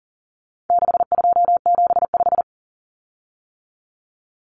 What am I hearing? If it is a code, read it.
Morse code, heard as a keyed tone, about 40 wpm, 700 Hz. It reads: 6275
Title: Common Cut Numbers - 4-Digit 40wpm